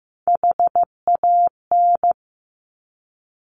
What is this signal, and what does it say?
Morse code, heard as a keyed tone, about 15 wpm, 700 Hz. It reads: HAN
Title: Top Three Letter Combinations in Words 15wpm